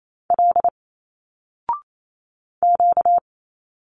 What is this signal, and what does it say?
Morse code, heard as a keyed tone, about 28 wpm, 700 Hz. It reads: L Q
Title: Single Letters 28wpm